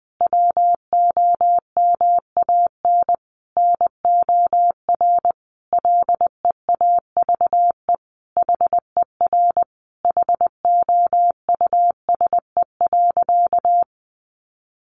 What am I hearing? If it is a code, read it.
Morse code, heard as a keyed tone, about 20 wpm, 700 Hz. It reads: WOMAN NOR LEAVE HER HOUSE.
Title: Sentences from Top 1500 Words 20wpm